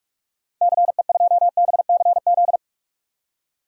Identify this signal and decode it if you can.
Morse code, heard as a keyed tone, about 45 wpm, 700 Hz. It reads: CE2BCZ
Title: DXCC-100 Call Signs - 6 Characters - 45wpm